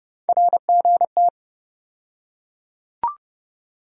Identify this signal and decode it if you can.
Morse code, heard as a keyed tone, about 29 wpm, 700 Hz. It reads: RGT
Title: USA Call Sign Postfixes - 3 Letters 30wpm